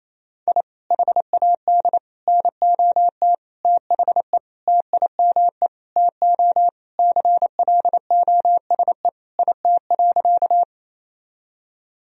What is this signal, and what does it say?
Morse code, heard as a keyed tone, about 28 wpm, 700 Hz. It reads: I HAD NOT THE TIME TO CLOSE IT.
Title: Sentences from Top 1000 Words - Review Part 1 - 28wpm